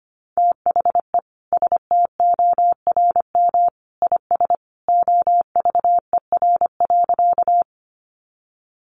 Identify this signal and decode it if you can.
Morse code, heard as a keyed tone, about 25 wpm, 700 Hz. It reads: THE STORM IS OVER.
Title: Sentences from Top 1500 Words 25wpm